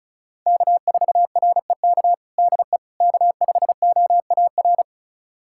Morse code, heard as a keyed tone, about 36 wpm, 700 Hz. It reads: K4REK DE K5OAR
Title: USA Callsign DE Callsign - 5 Characters - 35wpm